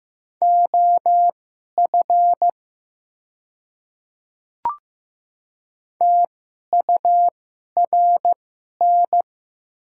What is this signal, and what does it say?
Morse code, heard as a keyed tone, about 15 wpm, spaced as for 11 wpm, 700 Hz. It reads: OF TURN
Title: 2x Character Spacing - Top 300 Words - 15wpm